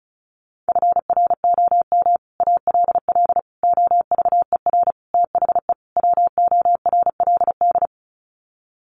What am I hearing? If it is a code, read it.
Morse code, heard as a keyed tone, about 34 wpm, 700 Hz. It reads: FROM ALL OVER THE WORLD
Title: Sets of 5 Words 35wpm